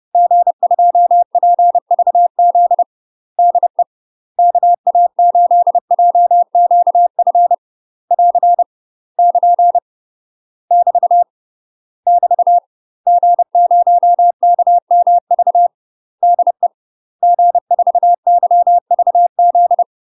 Morse code, heard as a keyed tone, about 30 wpm, 700 Hz. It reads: G2PVZ DE KA8JQF <AR> <KN> <BT> <BT> G0KMV DE G4YVZ
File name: UK MASTER QSOs TRAINING / 30WPM / UK MASTER QSOs 0026